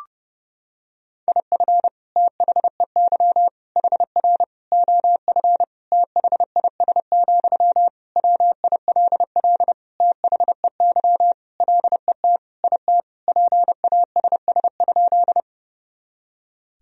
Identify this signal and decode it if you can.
Morse code, heard as a keyed tone, about 30 wpm, 700 Hz. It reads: IF THEY HR OF THIS, WILL THEY LET IT PASS?
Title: Sentences from Top 400 Words and Abbreviations 30wpm